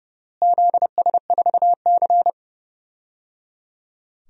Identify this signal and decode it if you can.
Morse code, heard as a keyed tone, about 30 wpm, 700 Hz. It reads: ZS4C
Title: DXCC-100 Call Signs - 4 Characters - Repeated 3x - 30wpm